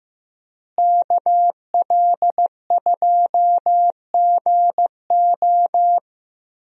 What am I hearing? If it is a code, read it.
Morse code, heard as a keyed tone, about 15 wpm, 700 Hz. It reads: KL2GO
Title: USA Call Signs - 5 Characters - Repeated 2x - 15wpm